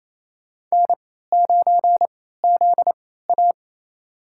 Morse code, heard as a keyed tone, about 28 wpm, spaced as for 16 wpm, 700 Hz. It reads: N9ZA
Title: USA Call Signs - 3x Character Spacing - 28wpm